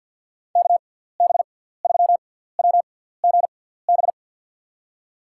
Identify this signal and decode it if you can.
Morse code, heard as a keyed tone, about 52 wpm, spaced as for 17 wpm, 700 Hz. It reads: KB3WGB
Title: USA Call Signs - 6 Characters - 6x Character Spacing - 50wpm